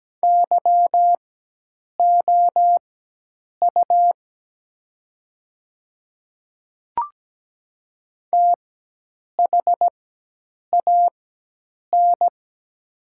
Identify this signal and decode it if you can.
Morse code, heard as a keyed tone, about 17 wpm, spaced as for 8 wpm, 700 Hz. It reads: YOU THAN
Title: 4x Character Spacing - Top 100 Words - 17wpm